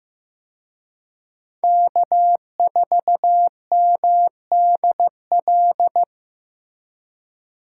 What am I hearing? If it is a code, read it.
Morse code, heard as a keyed tone, about 18 wpm, 700 Hz. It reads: K4MDL
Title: Speed Racing - USA 1x3 Call Signs 15wpm